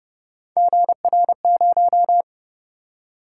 Morse code, heard as a keyed tone, about 28 wpm, 700 Hz. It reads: GR0